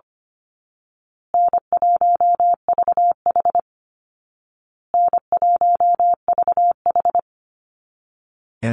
Morse code, heard as a keyed tone, about 25 wpm, 700 Hz. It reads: N1VH N1VH
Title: Call Signs Repeated 2x - 25wpm